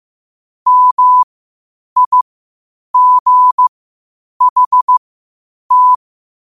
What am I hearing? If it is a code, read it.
Morse code, heard as a keyed tone, about 15 wpm, spaced as for 8 wpm, 1000 Hz. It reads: MIGHT
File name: qotdfiles / 8wpm